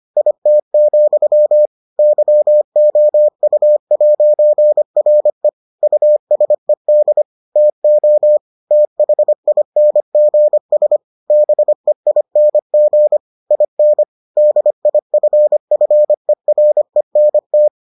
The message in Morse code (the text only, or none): IT, YOU'RE USED TO THINGS BEING IN DIFFERENT